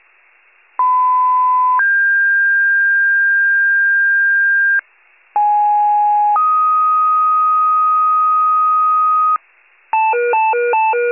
Two-tone page, heard as a one-second tone, then a three-second tone, about 1010 then 1600 Hz, about 830 then 1250 Hz, with noise. It is heard from 0.8 to 9.4 s.